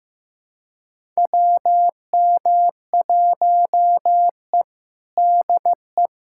Morse code, heard as a keyed tone, about 15 wpm, 700 Hz. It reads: WM1E DE
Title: USA Callsign DE Callsign - 4 Characters - 15wpm